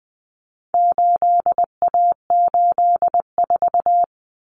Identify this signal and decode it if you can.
Morse code, heard as a keyed tone, about 20 wpm, 700 Hz. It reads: 8A84
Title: Common Cut Numbers - 4-Digit 20wpm